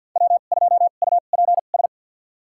Morse code, heard as a keyed tone, about 47 wpm, spaced as for 35 wpm, 700 Hz. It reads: W2UPS
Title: USA Call Signs - 5 Characters - 2x Character Spacing - 50wpm